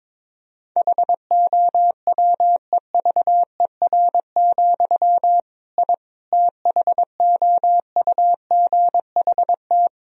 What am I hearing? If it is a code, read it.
Morse code, heard as a keyed tone, about 22 wpm, 700 Hz. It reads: HOWEVER, I THOUGHT
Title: Sentences from Top 1000 Words - Review Part 3 - 22wpm